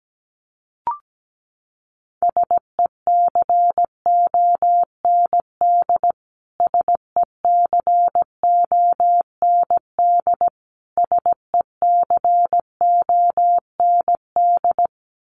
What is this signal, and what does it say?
Morse code, heard as a keyed tone, about 17 wpm, 700 Hz. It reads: SECOND SECOND SECOND
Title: Top 500 Words Repeated 17wpm